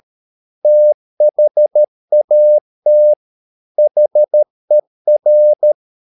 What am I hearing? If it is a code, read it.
Morse code, heard as a keyed tone, about 13 wpm, 600 Hz. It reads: THAT HER
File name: mp3 files / Lexiphiles - 13wpm